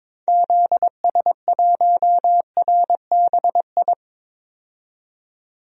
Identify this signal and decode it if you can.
Morse code, heard as a keyed tone, about 22 wpm, 700 Hz. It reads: ZS1RBI
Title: DXCC-100 Call Signs - 6 Characters - 22wpm